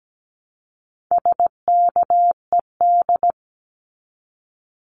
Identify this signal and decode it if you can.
Morse code, heard as a keyed tone, about 17 wpm, 700 Hz. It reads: SKED
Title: Abbreviations 17wpm